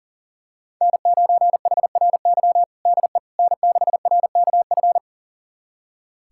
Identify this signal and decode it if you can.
Morse code, heard as a keyed tone, about 40 wpm, 700 Hz. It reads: N9HRY DE N6RKF
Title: USA Callsign DE Callsign - 5 Characters - 40wpm